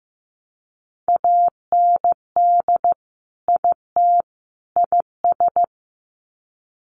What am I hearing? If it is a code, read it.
Morse code, heard as a keyed tone, about 15 wpm, 700 Hz. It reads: AND IT IS
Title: Sets of 3 Words 15wpm